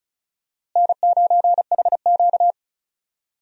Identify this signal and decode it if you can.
Morse code, heard as a keyed tone, about 34 wpm, 700 Hz. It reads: N9HQ
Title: USA Call Signs - 4 Characters - Repeated 3x - 35wpm